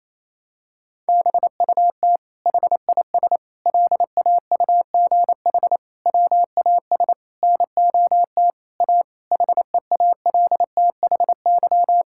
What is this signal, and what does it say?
Morse code, heard as a keyed tone, about 28 wpm, 700 Hz. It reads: BUT HIS LAUGH WAS NOT A HEALTHY
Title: Sentences from Top 1000 Words - Review Part 5 - 28wpm